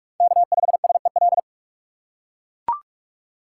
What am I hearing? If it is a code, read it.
Morse code, heard as a keyed tone, about 44 wpm, 700 Hz. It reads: K5SEL
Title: USA Call Signs - 6x Character Spacing - 45wpm